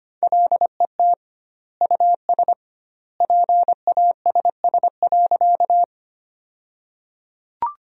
Morse code, heard as a keyed tone, about 25 wpm, 700 Hz. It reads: LET US PASS.
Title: 2x Word-Spacing - Sentences from Top 400 Words - 25wpm